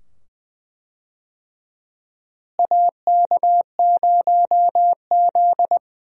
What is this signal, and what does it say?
Morse code, heard as a keyed tone, about 20 wpm, 700 Hz. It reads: AK0Z
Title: USA Call Signs - 4 Characters - Repeated 2x - 20wpm